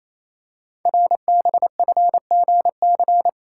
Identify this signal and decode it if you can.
Morse code, heard as a keyed tone, about 29 wpm, 700 Hz. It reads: RBFGC
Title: Code Groups - 5 Letters - 28wpm